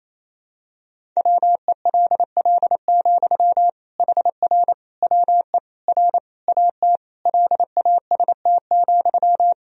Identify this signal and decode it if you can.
Morse code, heard as a keyed tone, about 28 wpm, 700 Hz. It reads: WELL, HR WE R AT LAST,
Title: Sentences from Top 300 Words and Abbreviations 28wpm